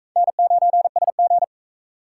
Morse code, heard as a keyed tone, about 44 wpm, 700 Hz. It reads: N9SG